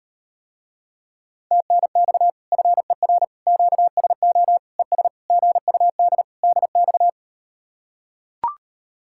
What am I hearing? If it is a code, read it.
Morse code, heard as a keyed tone, about 38 wpm, 700 Hz. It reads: TNX FER QSO ES GUD DX